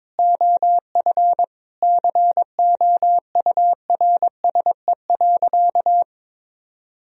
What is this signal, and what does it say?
Morse code, heard as a keyed tone, about 22 wpm, 700 Hz. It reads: OF COURSE.
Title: Sentences from Top 1000 Words - Review Part 3 - 22wpm